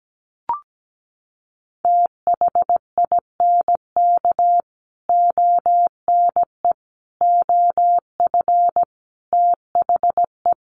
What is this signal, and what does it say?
Morse code, heard as a keyed tone, about 17 wpm, 700 Hz. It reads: THINK ONE OF THE
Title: Sets of 5 Words Encore2 17wpm